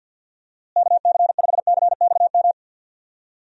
Morse code, heard as a keyed tone, about 52 wpm, 700 Hz. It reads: KC5CXM
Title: USA Call Signs - 6 Characters - Repeated 3x - 50wpm